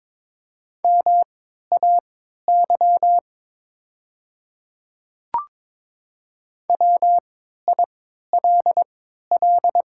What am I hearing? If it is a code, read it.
Morse code, heard as a keyed tone, about 22 wpm, spaced as for 13 wpm, 700 Hz. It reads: MAY WILL
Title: 3x Character Spacing - Top 200 Words - 22wpm